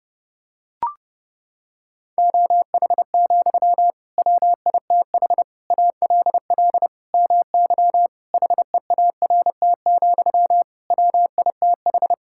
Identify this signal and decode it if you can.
Morse code, heard as a keyed tone, about 30 wpm, 700 Hz. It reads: OH, WITH ALL MY HEART, WITH
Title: Sentences from Top 1600 Words 30wpm